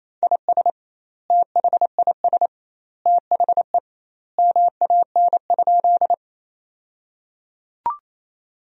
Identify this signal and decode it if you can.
Morse code, heard as a keyed tone, about 28 wpm, 700 Hz. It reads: IS THIS THE MAN?
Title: 2x Word-Spacing - Sentences from Top 200 Words - 28wpm